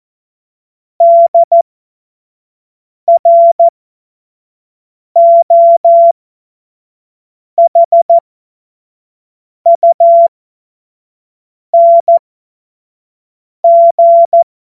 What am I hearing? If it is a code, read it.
Morse code, heard as a keyed tone, about 14 wpm, spaced as for 5 wpm, 680 Hz. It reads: DROHUNG
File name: Morsekurs / Witze / Witz 18. Juli 2023 25 F70